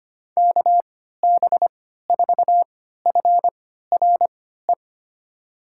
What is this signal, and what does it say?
Morse code, heard as a keyed tone, about 25 wpm, spaced as for 14 wpm, 700 Hz. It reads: KB4FRE
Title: USA Call Signs - 6 Characters - 3x Character Spacing - 25wpm